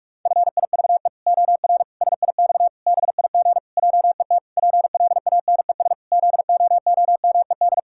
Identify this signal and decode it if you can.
Morse code, heard as a keyed tone, about 41 wpm, 700 Hz. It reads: FIVE OR SIX BIG JET PLANES ZOOMED
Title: Pangram Challenge v2 - 45wpm